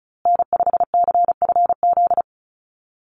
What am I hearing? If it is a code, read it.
Morse code, heard as a keyed tone, about 34 wpm, 700 Hz. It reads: N5CFZ
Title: USA Call Signs Repeated 2x 35wpm